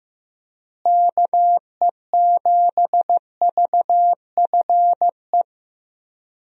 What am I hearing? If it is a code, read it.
Morse code, heard as a keyed tone, about 15 wpm, 700 Hz. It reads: KE7VFE
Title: USA Call Signs - 6 Characters - Repeated 3x - 15wpm